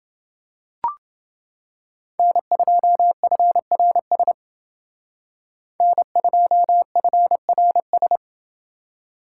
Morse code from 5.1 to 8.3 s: N2FRS

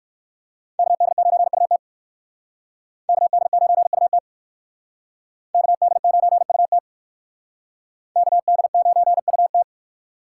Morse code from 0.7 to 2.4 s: KD9UT

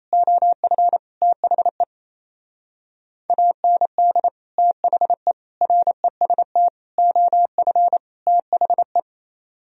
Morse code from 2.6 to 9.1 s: AND THE REST OF THE

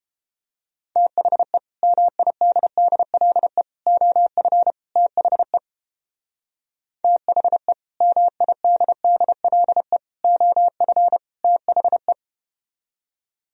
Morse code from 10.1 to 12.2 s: OF THE

